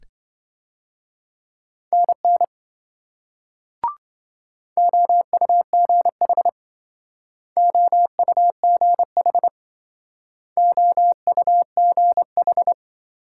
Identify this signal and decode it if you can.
Morse code, heard as a keyed tone, about 26 wpm, 700 Hz. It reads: NN OUGH OUGH OUGH